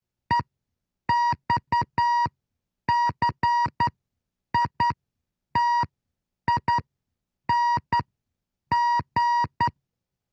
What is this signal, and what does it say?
Morse code, heard as a keyed tone, about 10 wpm, spaced as for 7 wpm, 950 Hz. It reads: EXCITING